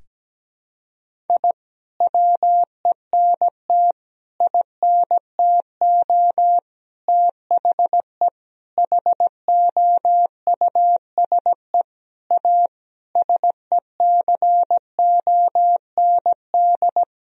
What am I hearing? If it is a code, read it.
Morse code, heard as a keyed tone, about 17 wpm, 700 Hz. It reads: I WENT INTO THE HOUSE A SECOND
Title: Sentences from Top 1000 Words - Review Part 1 - 17wpm